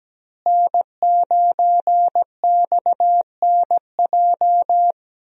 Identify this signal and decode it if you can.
Morse code, heard as a keyed tone, about 17 wpm, 700 Hz. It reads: N9XNJ
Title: USA Call Signs - 5 Characters - Repeated 3x - 17wpm